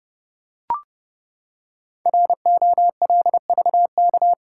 Morse code, heard as a keyed tone, about 30 wpm, 700 Hz. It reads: ROLVK
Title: Code Groups - 5 Letter:Numbers - 30wpm